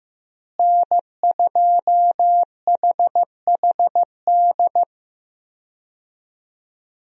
Morse code, heard as a keyed tone, about 15 wpm, 700 Hz. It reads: N2HHD
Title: USA Call Signs - 5 Characters - Repeated 2x - 15wpm